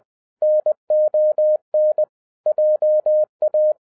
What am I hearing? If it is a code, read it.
Morse code, heard as a keyed tone, about 20 wpm, 610 Hz. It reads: NON JA